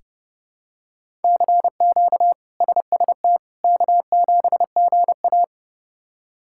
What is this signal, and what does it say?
Morse code, heard as a keyed tone, about 30 wpm, 700 Hz. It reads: CQ SST K7GA
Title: K1USN SST QSO Practice - 30wpm